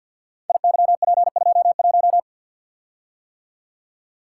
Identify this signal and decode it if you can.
Morse code, heard as a keyed tone, about 52 wpm, 700 Hz. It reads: IYP21
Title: Code Groups - 5 Letter:Numbers - 50wpm